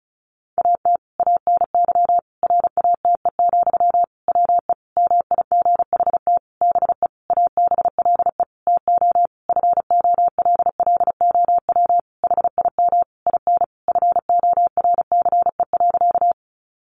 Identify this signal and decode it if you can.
Morse code, heard as a keyed tone, about 34 wpm, 700 Hz. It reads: AT ANY RATE, WE MIGHT BE ABLE TO FOLLOW HIM IN FORCE.
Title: Sentences from Top 700 Words 35wpm